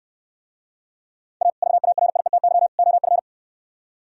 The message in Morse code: A VARIETY OF

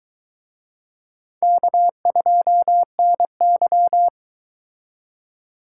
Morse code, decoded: K2NY

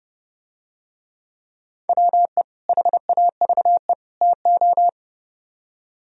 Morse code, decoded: WE HAVE TO

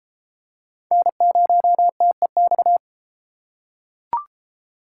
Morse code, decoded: N0TEX

